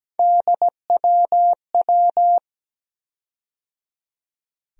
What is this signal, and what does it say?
Morse code, heard as a keyed tone, about 17 wpm, 700 Hz. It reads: DWW
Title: USA Call Sign Postfixes - 3 Letters 17wpm